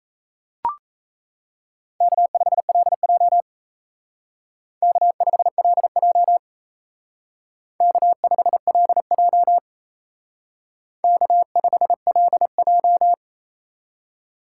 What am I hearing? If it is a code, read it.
Morse code, heard as a keyed tone, about 35 wpm, 700 Hz. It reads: K5LJ K5LJ K5LJ K5LJ